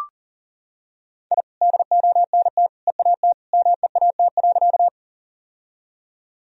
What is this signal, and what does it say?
Morse code, heard as a keyed tone, about 39 wpm, 700 Hz. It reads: I DONT EAT MEAT.
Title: Sentences from Top 1000 Words - Review Part 3 - 40wpm